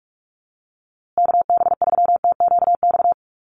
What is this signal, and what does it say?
Morse code, heard as a keyed tone, about 44 wpm, 700 Hz. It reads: KB3TQX
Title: USA Call Signs Repeated 2x 45wpm